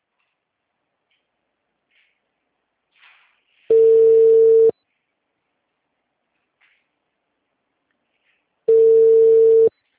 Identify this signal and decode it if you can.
Ringback tone (China, 450 Hz).
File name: out- -2042- - - .260